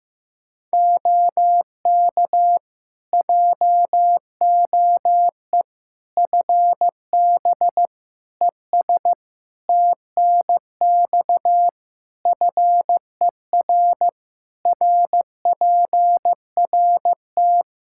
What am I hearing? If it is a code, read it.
Morse code, heard as a keyed tone, about 16 wpm, 700 Hz. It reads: OK JOE FB ES TNX FER RPRT
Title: K1USN SST QSO Practice 15wpm